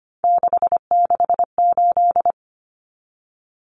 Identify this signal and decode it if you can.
Morse code, heard as a keyed tone, about 25 wpm, 700 Hz. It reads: 668
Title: Common Cut Numbers - 3-Digit 25wpm